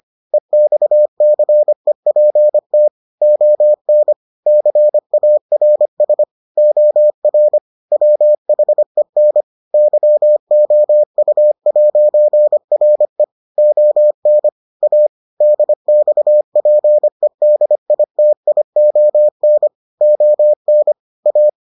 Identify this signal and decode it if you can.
Morse code, heard as a keyed tone, about 25 wpm, 600 Hz. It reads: EXCEPT ON CARS OR WHEN YOU'RE ON A DXPEDITION ON A